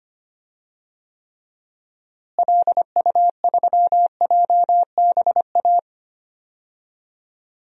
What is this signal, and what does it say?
Morse code, heard as a keyed tone, about 25 wpm, 700 Hz. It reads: LU3JBA
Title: DXCC-100 Call Signs - 6 Characters - Repeated 2x - 25wpm